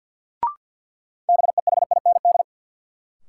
Morse code, heard as a keyed tone, about 52 wpm, 700 Hz. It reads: BEHIND
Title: Sentences from Top 500 Words - New Word Drill - 50wpm